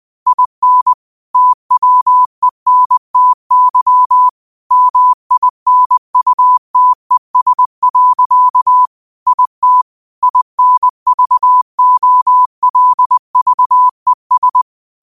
Morse code, heard as a keyed tone, about 20 wpm, 1000 Hz. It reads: IN TWENTY MINUTES. IT INVOLVES
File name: qotdfiles / 20wpm